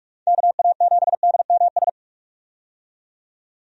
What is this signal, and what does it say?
Morse code, heard as a keyed tone, about 46 wpm, 700 Hz. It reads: KA7DMS
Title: USA Call Signs - 6 Characters - 45wpm